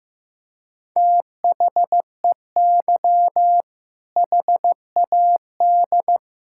Morse code, heard as a keyed tone, about 15 wpm, 700 Hz. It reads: THEY HAD
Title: Sentences from Top 1600 Words 15wpm